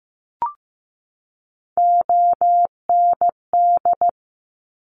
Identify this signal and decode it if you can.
Morse code, heard as a keyed tone, about 15 wpm, 700 Hz. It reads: OND
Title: Top Three Letter Combinations in Words 15wpm